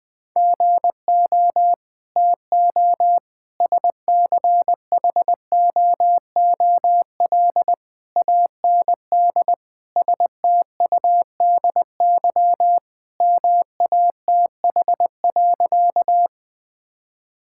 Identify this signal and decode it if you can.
Morse code, heard as a keyed tone, about 20 wpm, 700 Hz. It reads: GO TO SCHOOL AND STUDY MATH.
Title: Sentences from Top 1000 Words - Review Part 6 - 20wpm